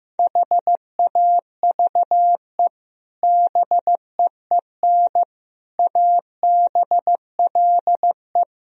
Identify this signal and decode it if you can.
Morse code, heard as a keyed tone, about 15 wpm, 700 Hz. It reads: HAVE BEEN ABLE
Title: Speed-Racing - Sets of 4 Words 15wpm